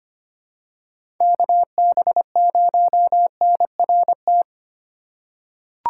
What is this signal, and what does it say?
Morse code, heard as a keyed tone, about 25 wpm, 700 Hz. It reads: KB0NRT
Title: USA Call Signs - 6 Characters - 3x Character Spacing - 25wpm